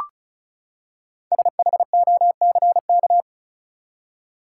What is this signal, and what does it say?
Morse code, heard as a keyed tone, about 36 wpm, 700 Hz. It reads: SHOCK
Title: Five Letter Words - 35wpm